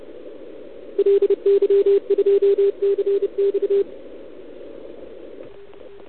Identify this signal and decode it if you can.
Morse code, heard as a keyed tone, about 32 wpm, 410 Hz. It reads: LY2CX
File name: ARCHIV / OKDX2019 / OKOM40 KX3